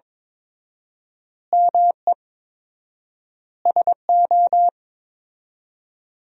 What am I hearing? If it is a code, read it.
Morse code, heard as a keyed tone, about 22 wpm, 690 Hz. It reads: ME SO
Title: 4x Word-Spacing - Sentences from Top 400 Words - 22wpm